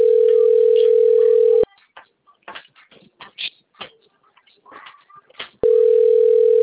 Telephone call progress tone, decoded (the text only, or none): ringback tone